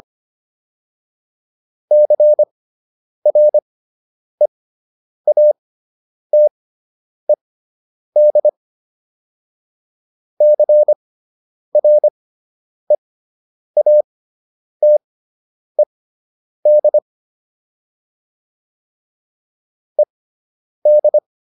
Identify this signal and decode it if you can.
Morse code, heard as a keyed tone, about 25 wpm, spaced as for 9 wpm, 600 Hz. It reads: CREATED CREATED ED